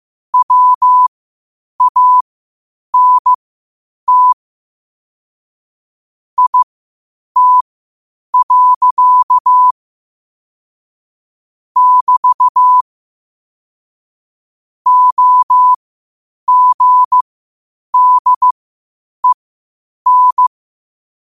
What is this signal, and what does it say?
Morse code, heard as a keyed tone, about 15 wpm, spaced as for 8 wpm, 1000 Hz. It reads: WANT IT. <BT> OGDEN
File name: qotdfiles / 8wpm